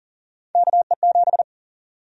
Morse code, heard as a keyed tone, about 40 wpm, 700 Hz. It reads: KE7
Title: USA Call Sign Prefixes - Repeated 2x - 40wpm